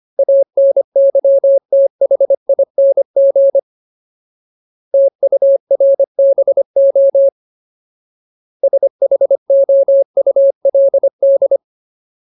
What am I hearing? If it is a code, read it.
Morse code, heard as a keyed tone, about 25 wpm, 550 Hz. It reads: ANYTHING TURBO SHOULD
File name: Ultimate Word List / 25WPM Ultimate